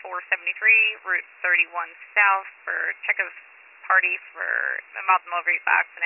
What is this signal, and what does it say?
Emergency Alert System (SAME) tones, heard as harsh buzzing bursts, with speech. It is heard from 1.0 to 4.9 s.